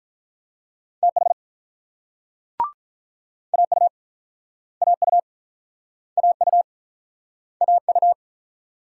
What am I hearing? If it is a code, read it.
Morse code, heard as a keyed tone, about 47 wpm, 700 Hz. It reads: TH AU AU AU AU